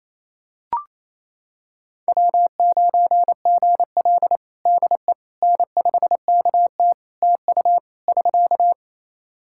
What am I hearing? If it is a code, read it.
Morse code, heard as a keyed tone, about 28 wpm, 700 Hz. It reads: W9GL DE N5KT TU <SK>
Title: Practice QSO - Exchange 4 - Ending - 28wpm